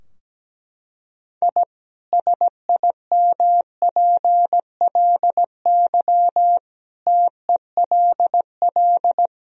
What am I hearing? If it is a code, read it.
Morse code, heard as a keyed tone, about 17 wpm, 700 Hz. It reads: I SIMPLY TELL
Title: Sentences from Top 1000 Words - Review Part 4 - 17wpm